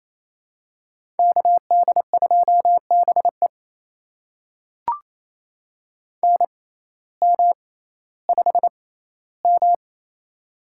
Morse code, heard as a keyed tone, about 28 wpm, spaced as for 15 wpm, 700 Hz. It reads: KD2BE NM5M
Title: USA Call Signs - 6x Character Spacing - 28wpm